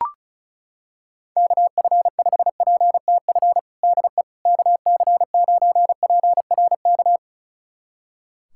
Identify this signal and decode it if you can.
Morse code, heard as a keyed tone, about 36 wpm, 700 Hz. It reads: KF5PTF DE KC9PRK
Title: USA Callsign DE Callsign - 6 Characters - 35wpm